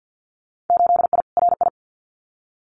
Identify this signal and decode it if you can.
Morse code, heard as a keyed tone, about 50 wpm, 700 Hz. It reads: 8I RI
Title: Winter Field Day Exchanges Encore 50wpm